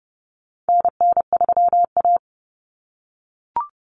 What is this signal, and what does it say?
Morse code, heard as a keyed tone, about 30 wpm, 700 Hz. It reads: NN3A
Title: Common Cut Numbers - 4-Digit 30wpm